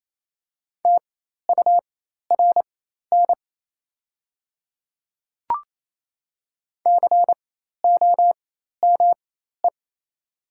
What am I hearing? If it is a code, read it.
Morse code, heard as a keyed tone, about 28 wpm, spaced as for 13 wpm, 700 Hz. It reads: TURN COME
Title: 4x Character Spacing - Top 200 Words - 28wpm